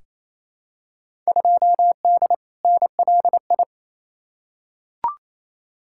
Morse code, heard as a keyed tone, about 28 wpm, 700 Hz. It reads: 2D NLI
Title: Field Day Exchanges Encore - 28wpm